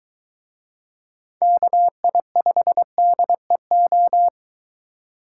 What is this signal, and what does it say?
Morse code, heard as a keyed tone, about 23 wpm, 700 Hz. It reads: KI5DEO